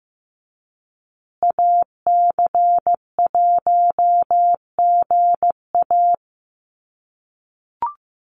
Morse code, heard as a keyed tone, about 15 wpm, 700 Hz. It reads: AC1GA
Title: USA Call Signs Repeated 2x 15wpm